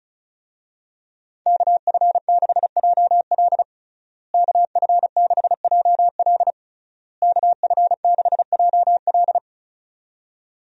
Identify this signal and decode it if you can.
Morse code, heard as a keyed tone, about 34 wpm, 700 Hz. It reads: KF6JL KF6JL KF6JL
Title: USA Call Signs - 5 Characters - Repeated 3x - 35wpm